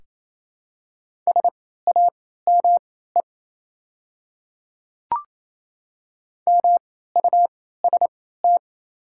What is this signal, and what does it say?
Morse code, heard as a keyed tone, about 28 wpm, spaced as for 16 wpm, 700 Hz. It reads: SAME MUST
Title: 3x Character Spacing - Top 300 Words - 28wpm